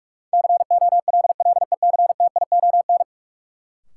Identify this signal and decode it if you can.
Morse code, heard as a keyed tone, about 44 wpm, 700 Hz. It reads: COLLECTION
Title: Sentences from Top 1200 Words - New Word Drill Alt - 45wpm